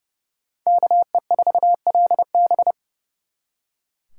Morse code, heard as a keyed tone, about 38 wpm, 700 Hz. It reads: KE4LB